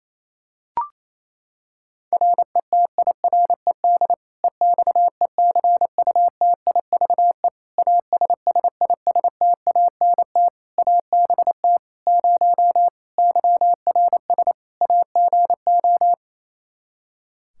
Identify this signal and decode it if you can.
Morse code, heard as a keyed tone, about 27 wpm, 700 Hz. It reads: RETIRED EXECUTIVE ASSISTANT ABT 0 YRS AGO
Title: K1USN SST QSO Practice 28wpm